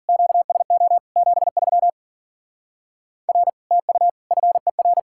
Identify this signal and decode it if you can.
Morse code, heard as a keyed tone, about 48 wpm, 700 Hz. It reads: QSO 73 R TU FER
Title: Speed-Racing Common QSO Phrases - 40wpm